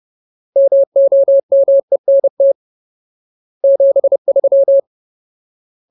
Morse code, heard as a keyed tone, about 30 wpm, 550 Hz. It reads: MOMENT 73
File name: Ultimate Word List / 30WPM Ultimate